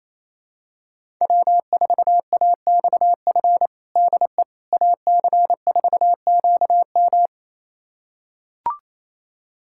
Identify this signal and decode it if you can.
Morse code, heard as a keyed tone, about 28 wpm, 700 Hz. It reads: W4AXF DE AC4QM
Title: USA Callsign DE Callsign - 5 Characters - 28wpm